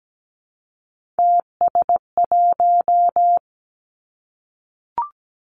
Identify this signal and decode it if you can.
Morse code, heard as a keyed tone, about 17 wpm, 700 Hz. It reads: TS1
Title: Three Challenging Letter-Number Combinations 17wpm